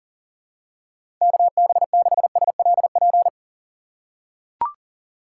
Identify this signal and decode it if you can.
Morse code, heard as a keyed tone, about 38 wpm, 700 Hz. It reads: KB6SLP
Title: USA Call Signs - 6 Characters - 3x Character Spacing - 40wpm